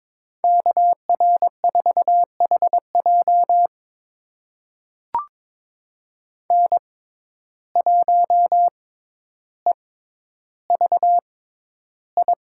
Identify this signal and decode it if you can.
Morse code, heard as a keyed tone, about 22 wpm, spaced as for 11 wpm, 700 Hz. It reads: KR4HJ N1EVI
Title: USA Call Signs - 5 Characters - 6x Character Spacing - 22wpm